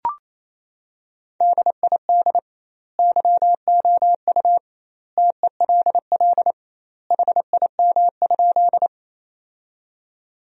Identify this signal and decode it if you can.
Morse code, heard as a keyed tone, about 28 wpm, 700 Hz. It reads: DID YOU TELL HIM?
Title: 2x Word-Spacing - Sentences from Top 200 Words - 28wpm